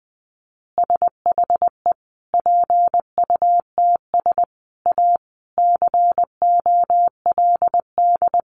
Morse code, heard as a keyed tone, about 20 wpm, 700 Hz. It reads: SHE PUTS A COLD
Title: Easy Phrases 20wpm